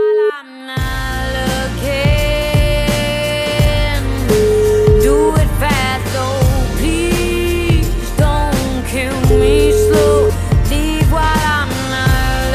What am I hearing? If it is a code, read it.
Ringback tone (Europe, 425 Hz), with noise.